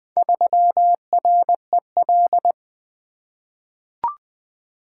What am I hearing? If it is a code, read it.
Morse code, heard as a keyed tone, about 20 wpm, 700 Hz. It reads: 3REL
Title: Four Letter-Number Combinations - 20wpm